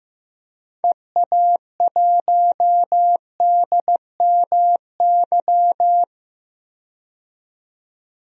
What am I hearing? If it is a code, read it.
Morse code, heard as a keyed tone, about 15 wpm, 700 Hz. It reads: EA1DMY
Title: DXCC-100 Call Signs - 6 Characters - 15wpm